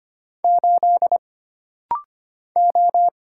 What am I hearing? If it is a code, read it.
Morse code, heard as a keyed tone, about 25 wpm, 700 Hz. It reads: 8 O
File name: Single Character (Letter:Number:Punctuation:Prosign) - Rapid-Fire - 25wpm